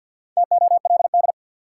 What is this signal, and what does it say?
Morse code, heard as a keyed tone, about 53 wpm, 700 Hz. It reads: TOLD
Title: Sentences from Top 400 Words - New Word Drill - 50wpm